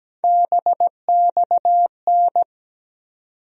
Morse code, heard as a keyed tone, about 17 wpm, 700 Hz. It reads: BXN
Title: USA Call Sign Postfixes - 3 Letters 17wpm